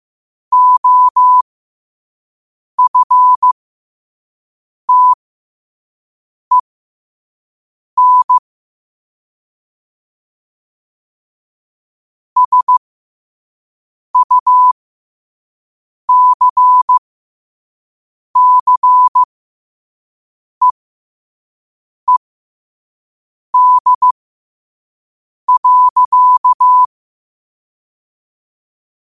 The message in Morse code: OFTEN SUCCEED.

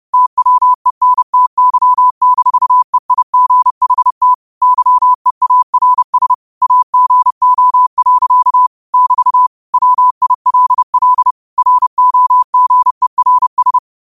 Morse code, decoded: TWENTY-EIGHT YEARS AGO. <BT> WILL ROGERS

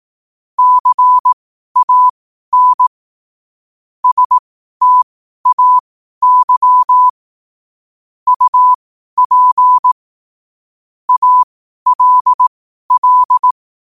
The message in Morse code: CAN STAY UP ALL

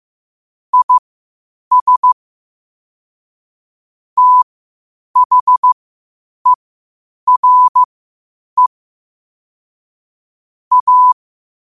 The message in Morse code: IS THERE A

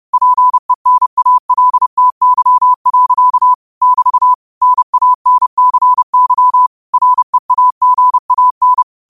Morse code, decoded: PENALTY. <BT> NANCY REAGAN

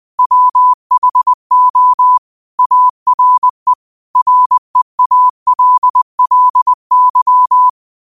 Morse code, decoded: WHO ARE REALLY

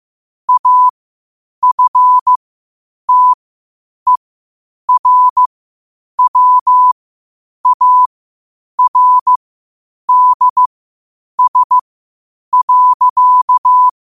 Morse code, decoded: AFTERWARDS.

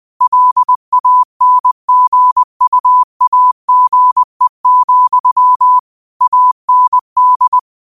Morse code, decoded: LANGUAGE, AND

